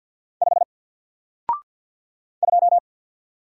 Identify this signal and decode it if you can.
Morse code, heard as a keyed tone, about 50 wpm, 700 Hz. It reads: 5 2
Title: Single Character (Letter:Number:Punctuation:Prosign) 50wpm